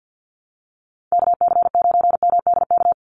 Morse code, heard as a keyed tone, about 50 wpm, 700 Hz. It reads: KC9MDK
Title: USA Call Signs 50wpm